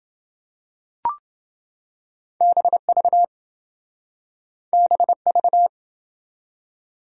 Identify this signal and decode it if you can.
Morse code, heard as a keyed tone, about 26 wpm, 700 Hz. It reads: BV BV